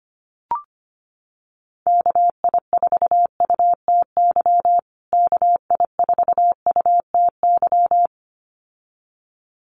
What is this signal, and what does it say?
Morse code, heard as a keyed tone, about 25 wpm, 700 Hz. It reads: KI4UTY KI4UTY
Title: USA Call Signs Repeated 2x 25wpm